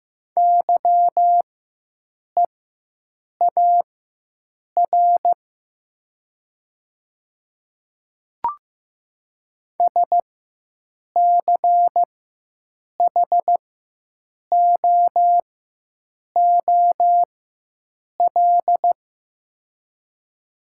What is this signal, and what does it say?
Morse code, heard as a keyed tone, about 15 wpm, spaced as for 7 wpm, 700 Hz. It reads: YEAR SCHOOL
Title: 4x Character Spacing - Top 300 Words - 15wpm